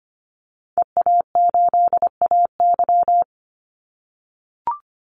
Morse code, heard as a keyed tone, about 25 wpm, 700 Hz. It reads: EA8AY
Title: Call Signs - 25wpm